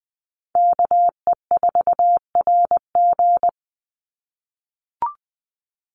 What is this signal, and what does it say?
Morse code, heard as a keyed tone, about 20 wpm, 700 Hz. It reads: KE4RG
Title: Call Signs - 20wpm